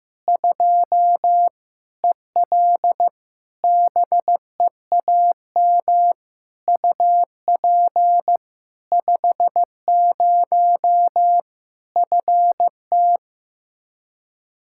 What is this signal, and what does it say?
Morse code, heard as a keyed tone, about 15 wpm, 700 Hz. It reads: 2 EL BEAM UP 50 FT
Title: Practice QSO - Exchange 2 - Info - 15wpm